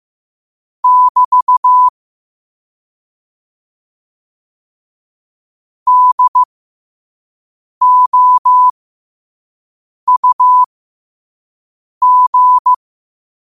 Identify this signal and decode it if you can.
Morse code, heard as a keyed tone, about 15 wpm, spaced as for 5 wpm, 1000 Hz. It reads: <BT> DOUG
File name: qotdfiles / 5wpm